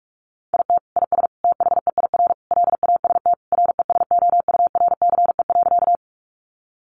Morse code, heard as a keyed tone, about 44 wpm, 700 Hz. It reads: IT IS THEIR LAST RESOURCE.
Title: Sentences from Top 1300 Words 45wpm